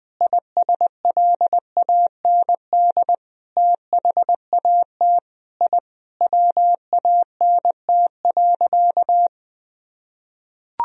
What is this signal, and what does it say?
Morse code, heard as a keyed tone, about 21 wpm, 700 Hz. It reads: ISLAND THAT I WANT.
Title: Sentences from Top 1000 Words - Review Part 3 - 20wpm